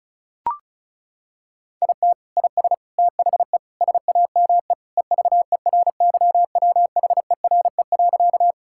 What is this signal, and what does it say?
Morse code, heard as a keyed tone, about 34 wpm, 700 Hz. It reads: IT IS THE SAME EVERYWHERE.
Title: Sentences from Top 1600 Words 35wpm